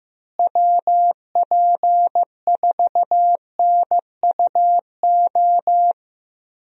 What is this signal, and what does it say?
Morse code, heard as a keyed tone, about 15 wpm, 700 Hz. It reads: WP4NUO
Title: USA Call Signs - 6 Characters - Repeated 3x - 15wpm